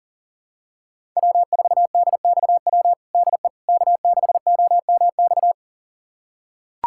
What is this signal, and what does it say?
Morse code, heard as a keyed tone, about 40 wpm, 700 Hz. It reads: W4DXW DE K6OMX
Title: USA Callsign DE Callsign - 5 Characters - 40wpm